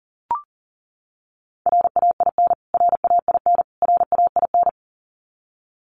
Morse code, heard as a keyed tone, about 40 wpm, 700 Hz. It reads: RAIN RAIN RAIN
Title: Top 300 Words Repeated 40wpm